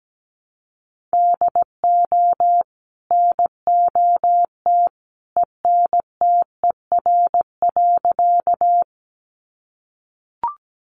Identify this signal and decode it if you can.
Morse code, heard as a keyed tone, about 17 wpm, 700 Hz. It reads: DO NOT ENTER.
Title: Sentences from Top 1300 Words 17wpm